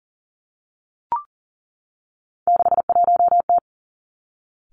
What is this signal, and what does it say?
Morse code, heard as a keyed tone, about 39 wpm, 700 Hz. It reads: 61T
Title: Three Challenging Letter-Number Combinations 40wpm